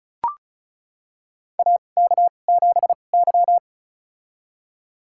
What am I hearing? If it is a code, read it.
Morse code, heard as a keyed tone, about 34 wpm, spaced as for 25 wpm, 700 Hz. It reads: AK7Y
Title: USA Call Signs - 4 Characters - 2x Character Spacing - 35wpm